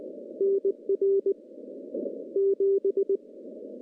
Morse code, heard as a keyed tone, about 20 wpm, 390 Hz. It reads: NR 7
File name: contests / CW-SS-k5tr- -0100